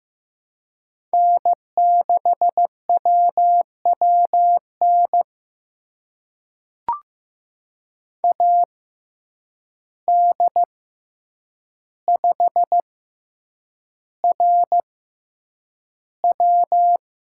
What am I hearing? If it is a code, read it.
Morse code, heard as a keyed tone, about 15 wpm, spaced as for 8 wpm, 700 Hz. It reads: N6WWN AD5RW
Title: USA Call Signs - 5 Characters - 6x Character Spacing - 15wpm